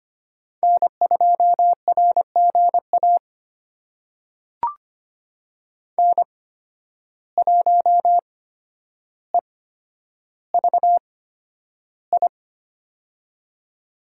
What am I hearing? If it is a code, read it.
Morse code, heard as a keyed tone, about 25 wpm, spaced as for 11 wpm, 700 Hz. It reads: N2RGA N1EVI
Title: USA Call Signs - 5 Characters - 8x Character Spacing - 25wpm